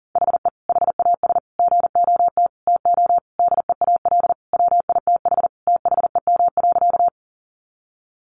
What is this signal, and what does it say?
Morse code, heard as a keyed tone, about 40 wpm, 700 Hz. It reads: HE HAS GOT TO DEAL WITH THEM.
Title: Sentences from Top 800 Words 40wpm